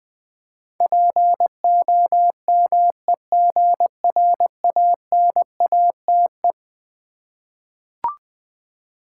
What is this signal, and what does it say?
Morse code, heard as a keyed tone, about 20 wpm, 700 Hz. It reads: POMEGRANATE
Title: Difficult Words with Hints - 20wpm